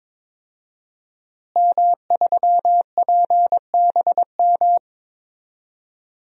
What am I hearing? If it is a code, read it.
Morse code, heard as a keyed tone, about 22 wpm, 700 Hz. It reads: M3PBM
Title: DXCC-100 Call Signs - 5 Characters - Repeated 2x - 22wpm